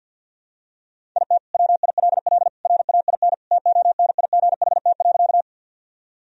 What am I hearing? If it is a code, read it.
Morse code, heard as a keyed tone, about 50 wpm, 700 Hz. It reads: IT WILL RAIN TONIGHT.
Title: Sentences from Top 1000 Words - Review Part 1 - 50wpm